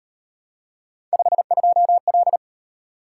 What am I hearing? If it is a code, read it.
Morse code, heard as a keyed tone, about 38 wpm, 700 Hz. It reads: 52L